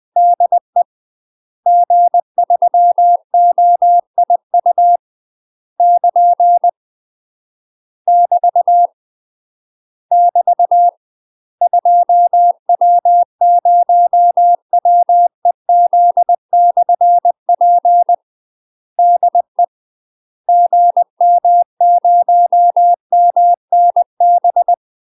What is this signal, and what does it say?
Morse code, heard as a keyed tone, about 20 wpm, 700 Hz. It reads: DE G3OIU <KN> <BT> <BT> 2W0WEZ/P DE GM0MNB
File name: UK MASTER QSOs TRAINING / 20WPM / UK MASTER QSOs 0005